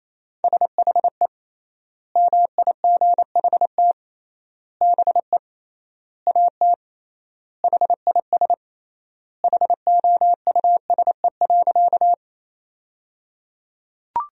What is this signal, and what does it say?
Morse code, heard as a keyed tone, about 27 wpm, 700 Hz. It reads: SHE MIGHT BE AT HIS HOUSE.
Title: 3x Word-Spacing - Sentences from Top 300 Words - 28wpm